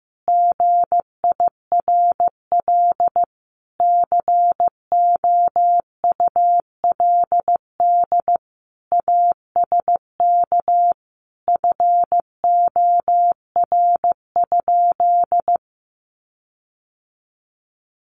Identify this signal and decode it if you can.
Morse code, heard as a keyed tone, about 15 wpm, 700 Hz. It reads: GIRL COULD ASK FOR?
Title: Sentences from Top 1400 Words 15wpm